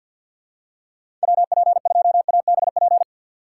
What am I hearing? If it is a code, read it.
Morse code, heard as a keyed tone, about 53 wpm, 700 Hz. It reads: WP2ABP
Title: USA Call Signs - 6 Characters - 8x Character Spacing - 50wpm